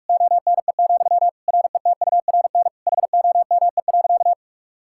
Morse code, heard as a keyed tone, about 46 wpm, 700 Hz. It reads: ONE, RETURN HOME.
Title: Sentences from Top 1000 Words - Review Part 5 - 45wpm